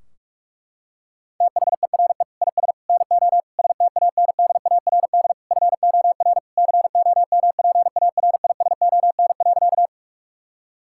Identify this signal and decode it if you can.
Morse code, heard as a keyed tone, about 46 wpm, 700 Hz. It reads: THERE IS NO STANDARD FOR COMPARISON.
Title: Sentences from Top 1600 Words 45wpm